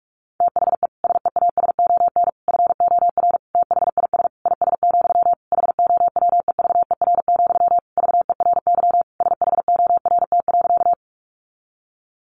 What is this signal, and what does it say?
Morse code, heard as a keyed tone, about 44 wpm, 700 Hz. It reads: THE SEASON FOR THIS IS, HOWEVER, VERY SHORT.
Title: Sentences from Top 1400 Words 45wpm